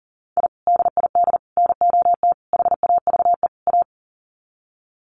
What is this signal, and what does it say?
Morse code, heard as a keyed tone, about 40 wpm, 700 Hz. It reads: I DID NOT HAVE A
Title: Sets of 5 Words 40wpm